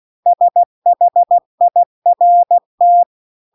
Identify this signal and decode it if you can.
Morse code, heard as a keyed tone, about 16 wpm, 700 Hz. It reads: SHIRT